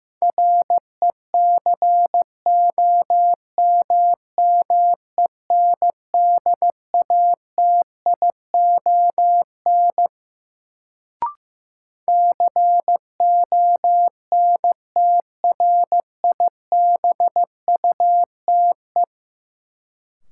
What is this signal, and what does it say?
Morse code, heard as a keyed tone, about 15 wpm, 700 Hz. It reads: RECOMMENDATION CONTRIBUTE
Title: Sentences from Top 1400 Words - New Word Drill - 15wpm